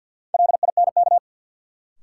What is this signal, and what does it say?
Morse code, heard as a keyed tone, about 52 wpm, 700 Hz. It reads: LINK
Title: Sentences from Top 1000 Words - Review Part 7 - New Word Drill - 50wpm